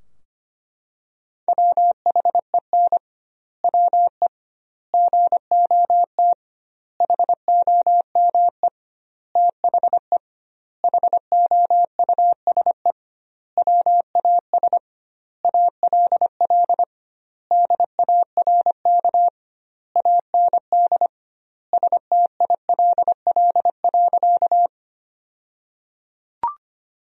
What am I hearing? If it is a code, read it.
Morse code, heard as a keyed tone, about 25 wpm, 700 Hz. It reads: WHEN WE GOT HOME THE HOUSE WAS ALL DARK AND STILL.
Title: 2x Word-Spacing - Sentences from Top 500 Words - 25wpm